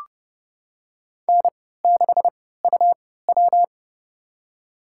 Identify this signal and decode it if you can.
Morse code, heard as a keyed tone, about 30 wpm, spaced as for 17 wpm, 700 Hz. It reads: N6UW
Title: USA Call Signs - 3x Character Spacing - 30wpm